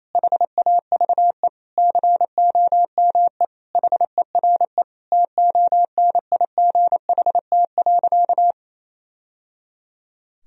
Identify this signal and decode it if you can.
Morse code, heard as a keyed tone, about 28 wpm, 700 Hz. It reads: HAVE COME HERE TONIGHT.
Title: Sentences from Top 1000 Words - Review Part 3 - 28wpm